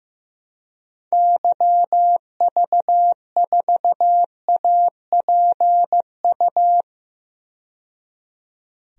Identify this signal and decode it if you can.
Morse code, heard as a keyed tone, about 15 wpm, 700 Hz. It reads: YV4APU
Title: DXCC-100 Call Signs - 6 Characters - 15wpm